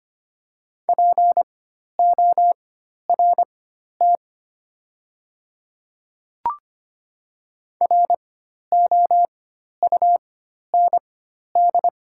Morse code, heard as a keyed tone, about 25 wpm, spaced as for 12 wpm, 700 Hz. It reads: PORT ROUND
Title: 4x Character Spacing - Top 200 Words - 25wpm